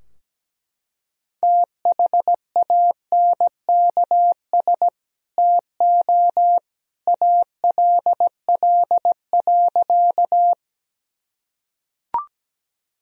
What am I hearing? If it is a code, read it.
Morse code, heard as a keyed tone, about 17 wpm, 700 Hz. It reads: THANKS TO ALL.
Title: Sentences from Top 1000 Words - Review Part 3 - 17wpm